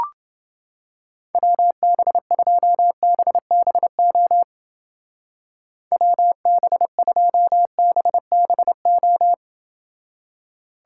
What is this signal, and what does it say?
Morse code, heard as a keyed tone, about 28 wpm, 700 Hz. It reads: WB2BBO WB2BBO